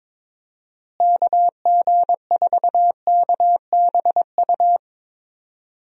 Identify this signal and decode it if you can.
Morse code, heard as a keyed tone, about 22 wpm, 700 Hz. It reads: KG4KBU
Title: USA Call Signs - 6 Characters - Repeated 2x - 22wpm